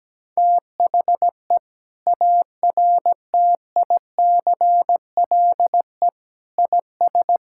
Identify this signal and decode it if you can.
Morse code, heard as a keyed tone, about 17 wpm, 700 Hz. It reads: THE ARTICLE IS
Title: Sentences from Top 1000 Words - Review Part 5 - 17wpm